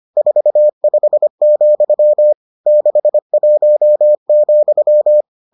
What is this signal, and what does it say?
Morse code, heard as a keyed tone, about 25 wpm, 600 Hz. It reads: 45, 61,